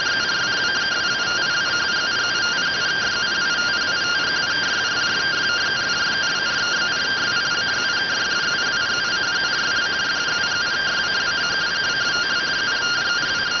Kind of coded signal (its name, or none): two-tone teleprinter signal